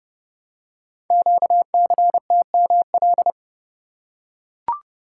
Morse code, heard as a keyed tone, about 30 wpm, 700 Hz. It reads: QCTML
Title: Code Groups - 5 Letters - 30wpm